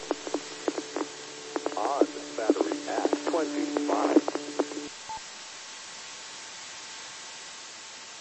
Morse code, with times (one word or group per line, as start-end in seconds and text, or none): none